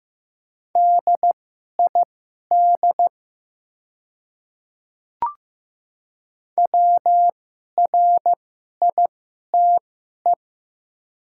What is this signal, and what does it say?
Morse code, heard as a keyed tone, about 15 wpm, spaced as for 11 wpm, 700 Hz. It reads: DID WRITE
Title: 2x Character Spacing - Top 100 Words - 15wpm